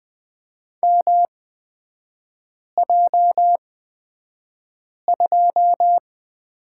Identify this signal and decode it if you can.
Morse code, heard as a keyed tone, about 20 wpm, spaced as for 5 wpm, 700 Hz. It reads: MJ2
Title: 250 Lesson 40e - BK - Three Characters 20wpm